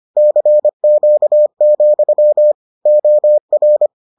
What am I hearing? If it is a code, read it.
Morse code, heard as a keyed tone, about 25 wpm, 600 Hz. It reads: CQ, OR